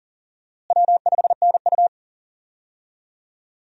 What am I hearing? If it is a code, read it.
Morse code, heard as a keyed tone, about 40 wpm, 700 Hz. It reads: W5NU
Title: USA Call Signs - 4 Characters - Repeated 2x - 40wpm